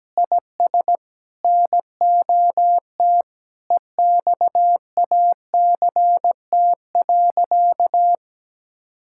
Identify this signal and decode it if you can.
Morse code, heard as a keyed tone, about 17 wpm, 700 Hz. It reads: IS NOT EXACT.
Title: Sentences from Top 1600 Words 17wpm